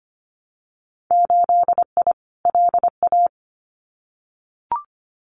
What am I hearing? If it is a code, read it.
Morse code, heard as a keyed tone, about 25 wpm, 700 Hz. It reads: 8I LA
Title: Winter Field Day Exchanges Encore 25wpm